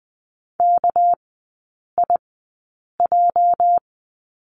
Morse code, heard as a keyed tone, about 19 wpm, spaced as for 8 wpm, 690 Hz. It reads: KIJ
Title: 205 Lesson 33e - Letter j - Three Characters 20wpm